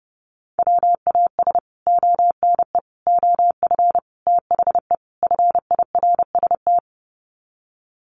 Morse code, heard as a keyed tone, about 30 wpm, 700 Hz. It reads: WAS ONE OF THE FIRST
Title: Sets of 5 Words 30wpm